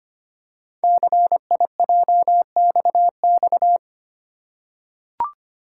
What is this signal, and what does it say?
Morse code, heard as a keyed tone, about 25 wpm, 700 Hz. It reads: CIJXX
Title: Code Groups - 5 Letters - 25wpm